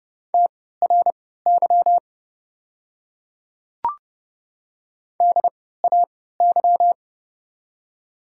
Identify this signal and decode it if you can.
Morse code, heard as a keyed tone, about 30 wpm, spaced as for 17 wpm, 700 Hz. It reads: TRY DAY
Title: 3x Character Spacing - Top 200 Words - 30wpm